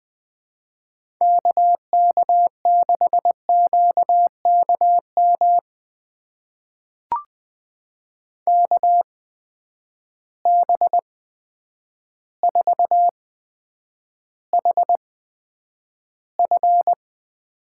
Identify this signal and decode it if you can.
Morse code, heard as a keyed tone, about 20 wpm, spaced as for 9 wpm, 700 Hz. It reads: KK6QKM KB4HF
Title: USA Call Signs - 6 Characters - 8x Character Spacing - 20wpm